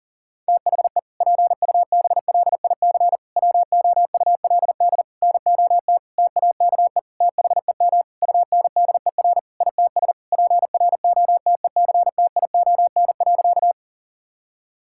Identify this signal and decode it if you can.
Morse code, heard as a keyed tone, about 40 wpm, 700 Hz. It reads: THE PUBLIC WOULD NOT TAKE THEM UNDER ITS PROTECTION.
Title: Sentences from Top 1600 Words 40wpm